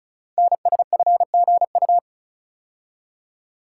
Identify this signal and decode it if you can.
Morse code, heard as a keyed tone, about 36 wpm, 700 Hz. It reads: NSFGU
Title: Code Groups - 5 Letters - 35wpm